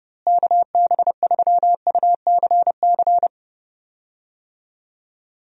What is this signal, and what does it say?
Morse code, heard as a keyed tone, about 32 wpm, 700 Hz. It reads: KB3UCC
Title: USA Call Signs - 6 Characters - Repeated 3x - 30wpm